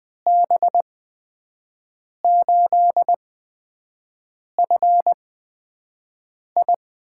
Morse code, heard as a keyed tone, about 20 wpm, spaced as for 5 wpm, 700 Hz. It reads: B8FI
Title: USA Call Signs - 6 Characters - 8x Character Spacing - 20wpm